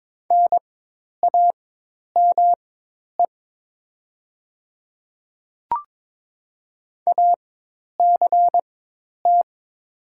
Morse code, heard as a keyed tone, about 22 wpm, spaced as for 10 wpm, 700 Hz. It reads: NAME ACT
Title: 4x Character Spacing - Top 200 Words - 22wpm